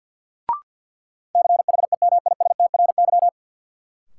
Morse code, heard as a keyed tone, about 53 wpm, 700 Hz. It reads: CHEMISTRY
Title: Sentences from Top 1000 Words - Review Part 6 - New Word Drill - 50wpm